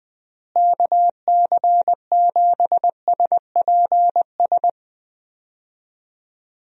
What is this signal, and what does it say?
Morse code, heard as a keyed tone, about 20 wpm, 700 Hz. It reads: KC7SPS
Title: USA Call Signs - 6 Characters - Repeated 3x - 20wpm